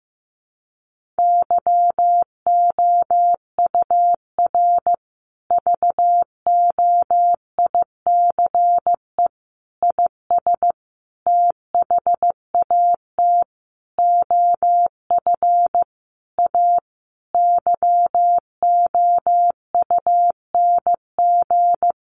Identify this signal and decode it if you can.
Morse code, heard as a keyed tone, about 15 wpm, 700 Hz. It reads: YOUR VOICE IS THAT OF A YOUNG
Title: Sentences from Top 500 Words 15wpm